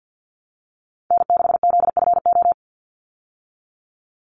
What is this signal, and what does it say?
Morse code, heard as a keyed tone, about 50 wpm, 700 Hz. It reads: N6ZFO
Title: Call Signs Repeated 3x - 50wpm